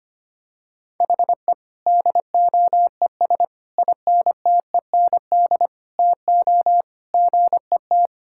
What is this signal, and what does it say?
Morse code, heard as a keyed tone, about 25 wpm, 700 Hz. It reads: HE DOES INTEND TO GET
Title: Sentences from Top 1600 Words 25wpm